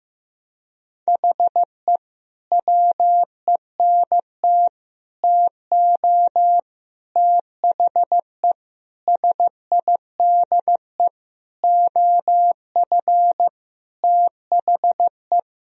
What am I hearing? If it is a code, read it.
Morse code, heard as a keyed tone, about 15 wpm, 700 Hz. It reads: HE WENT TO THE SIDE OF THE
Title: Sentences from Top 1000 Words - Review Part 1 - 15wpm